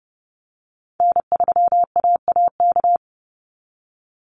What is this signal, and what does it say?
Morse code, heard as a keyed tone, about 30 wpm, 700 Hz. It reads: N3AAK
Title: Call Signs Repeated 3x - 30wpm